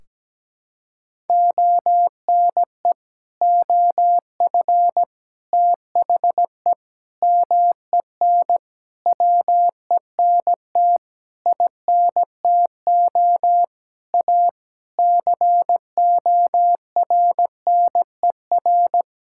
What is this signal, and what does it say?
Morse code, heard as a keyed tone, about 17 wpm, 700 Hz. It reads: ONE OF THE MEN WENT INTO A CORNER
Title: Sentences from Top 1600 Words 17wpm